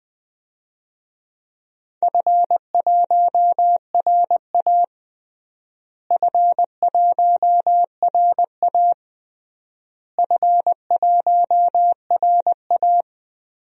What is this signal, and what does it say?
Morse code, heard as a keyed tone, about 20 wpm, 700 Hz. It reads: F1RA F1RA F1RA
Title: DXCC-100 Call Signs - 4 Characters - Repeated 3x - 20wpm